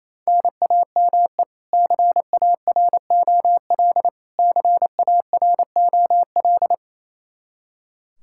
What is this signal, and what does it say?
Morse code, heard as a keyed tone, about 28 wpm, 700 Hz. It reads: NAME CAROL CAROL
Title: Practice QSO - Exchange 1 - Beginning - 28wpm